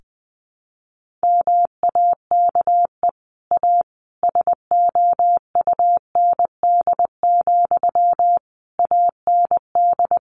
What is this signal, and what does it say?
Morse code, heard as a keyed tone, about 19 wpm, 700 Hz. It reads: MAKE A SOUND, AND
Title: Sentences from Top 1400 Words 20wpm